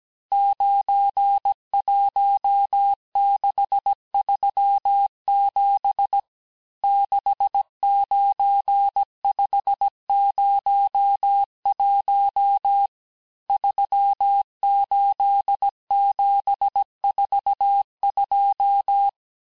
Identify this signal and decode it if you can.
Morse code, heard as a keyed tone, about 17 wpm, 800 Hz. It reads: 91637 69501 38742
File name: corrige-cours-1581-texte-chiffre-vitesse-16-mpm